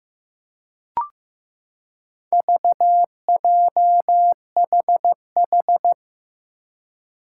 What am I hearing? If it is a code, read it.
Morse code, heard as a keyed tone, about 15 wpm, 700 Hz. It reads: VJHH
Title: Four Challenging Letter-Number Combinations 15wpm